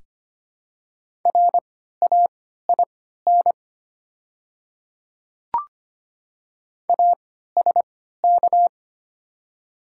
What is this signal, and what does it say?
Morse code, heard as a keyed tone, about 25 wpm, spaced as for 14 wpm, 700 Hz. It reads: RAIN ASK
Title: 3x Character Spacing - Top 300 Words - 25wpm